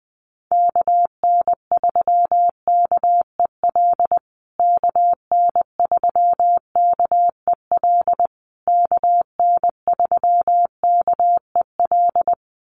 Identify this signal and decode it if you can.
Morse code, heard as a keyed tone, about 20 wpm, 700 Hz. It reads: KN3KEL KN3KEL KN3KEL
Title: USA Call Signs Repeated 3x 20wpm